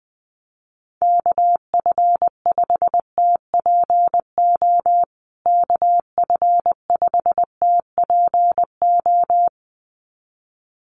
Morse code, heard as a keyed tone, about 20 wpm, 700 Hz. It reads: KF5TPO KF5TPO
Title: USA Call Signs Repeated 2x 20wpm